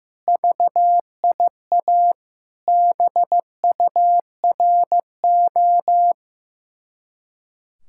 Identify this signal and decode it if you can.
Morse code, heard as a keyed tone, about 15 wpm, 700 Hz. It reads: VIA BURO
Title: Speed-Racing Common QSO Phrases Encore - 15wpm